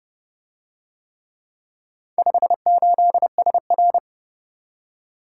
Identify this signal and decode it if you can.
Morse code, heard as a keyed tone, about 26 wpm, 700 Hz. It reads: 58SR